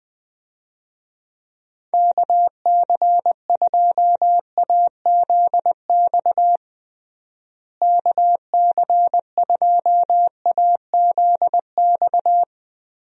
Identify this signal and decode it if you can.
Morse code, heard as a keyed tone, about 20 wpm, 700 Hz. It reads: KC2AZX KC2AZX
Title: USA Call Signs - 6 Characters - Repeated 3x - 20wpm